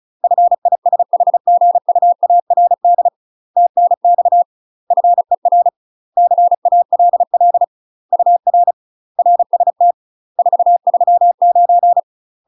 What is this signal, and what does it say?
Morse code, heard as a keyed tone, about 36 wpm, 700 Hz. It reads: FISHGUARD TNX FER CALL UR RST 439
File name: UK MASTER QSOs TRAINING / 35WPM / UK MASTER QSOs 0007